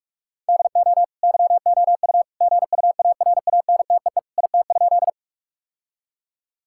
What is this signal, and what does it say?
Morse code, heard as a keyed tone, about 46 wpm, 700 Hz. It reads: DO YOU GUARANTEE IT?
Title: Sentences from Top 1600 Words 45wpm